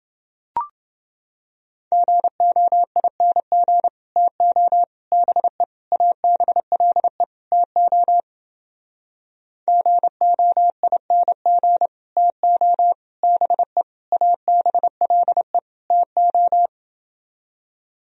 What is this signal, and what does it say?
Morse code, heard as a keyed tone, about 30 wpm, 700 Hz. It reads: GOING TO BE ABLE TO GOING TO BE ABLE TO